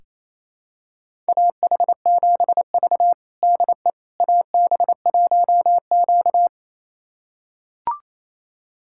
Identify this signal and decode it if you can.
Morse code, heard as a keyed tone, about 27 wpm, 700 Hz. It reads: AH7V DE AB1Q
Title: USA Callsign DE Callsign - 4 Characters - 28wpm